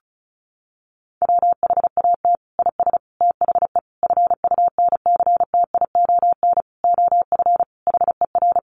Morse code, heard as a keyed tone, about 34 wpm, 700 Hz. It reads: WHAT IS THE FUNCTION OF HER
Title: Sentences from Top 1100 Words 35wpm